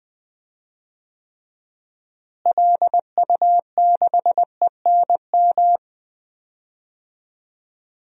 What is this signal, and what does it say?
Morse code, heard as a keyed tone, about 20 wpm, 700 Hz. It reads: LU6ENM
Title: DXCC-100 Call Signs - 6 Characters - Repeated 3x - 20wpm